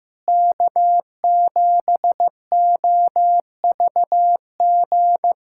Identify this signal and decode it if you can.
Morse code, heard as a keyed tone, about 16 wpm, 700 Hz. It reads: K7OVG
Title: USA Callsign DE Callsign 15wpm